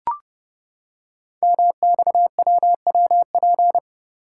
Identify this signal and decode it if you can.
Morse code, heard as a keyed tone, about 28 wpm, 700 Hz. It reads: MXWWP
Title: Code Groups - 5 Letters - 30wpm